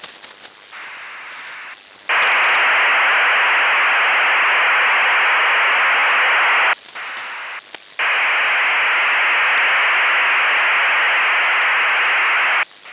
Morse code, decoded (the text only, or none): none